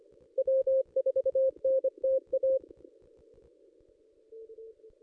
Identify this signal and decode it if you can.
Morse code, heard as a keyed tone, about 25 wpm, 530 Hz. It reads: W4NTA
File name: contests / ss-cw-k5tr-op-n5rz -1800